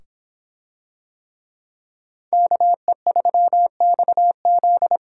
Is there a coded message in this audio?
Yes: Morse code from 2.3 to 5.0 s, heard as a keyed tone, about 25 wpm, 700 Hz. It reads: KE3XZ